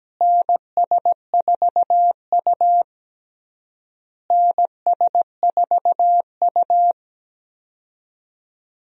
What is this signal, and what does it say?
Morse code, heard as a keyed tone, about 17 wpm, 700 Hz. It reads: NS4U NS4U
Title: USA Call Signs - 4 Characters - Repeated 3x - 17wpm